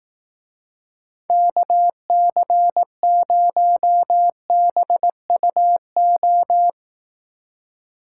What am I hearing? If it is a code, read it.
Morse code, heard as a keyed tone, about 18 wpm, 700 Hz. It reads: KC0BUO
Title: Speed Racing - USA 2x3 Call Signs 15wpm